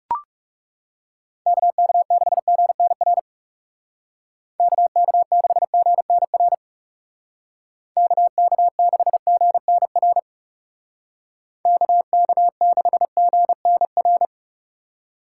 Morse code, heard as a keyed tone, about 40 wpm, 700 Hz. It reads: KK6GNR KK6GNR KK6GNR KK6GNR